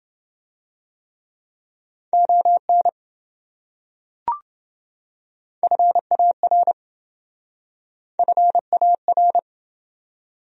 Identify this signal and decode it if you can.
Morse code, heard as a keyed tone, about 27 wpm, 700 Hz. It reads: ON FAR FAR